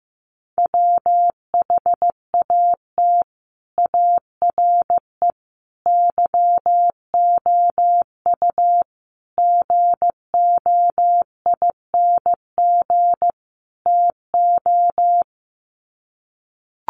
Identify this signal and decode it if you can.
Morse code, heard as a keyed tone, about 15 wpm, 700 Hz. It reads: WHAT ARE YOU GOING TO
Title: Sets of 5 Words 15wpm